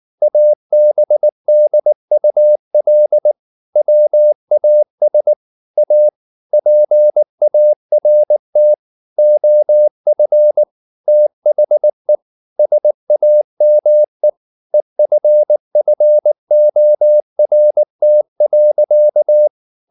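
Morse code, heard as a keyed tone, about 19 wpm, 600 Hz. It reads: ABDUL WAS A PART OF THE SAME EFFORT.
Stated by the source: Morse code at 19 wpm from CW med Gnister, reading Ken Follett's novel Never in English.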